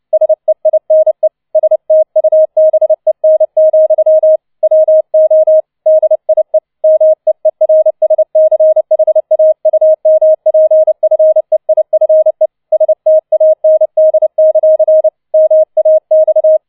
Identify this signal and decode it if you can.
Morse code, heard as a keyed tone, about 29 wpm, 630 Hz. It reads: SEINE STUBEN, WO DIE MEERSCHAUMPFEIFE STAND; MAX